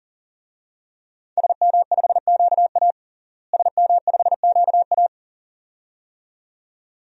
Morse code, heard as a keyed tone, about 40 wpm, 690 Hz. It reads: SM5QA SM5QA
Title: DXCC-100 Call Signs - 5 Characters - Repeated 2x - 40wpm